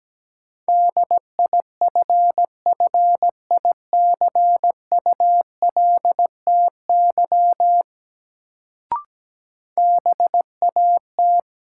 Morse code, heard as a keyed tone, about 17 wpm, 700 Hz. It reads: DIFFICULTY BAT
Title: Sentences from Top 1500 Words - New Word Drill - 17wpm